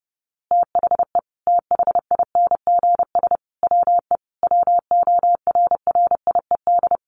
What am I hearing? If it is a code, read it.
Morse code, heard as a keyed tone, about 30 wpm, 700 Hz. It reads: THE THINGS WE WORRIED
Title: A Sound Of Thunder 30wpm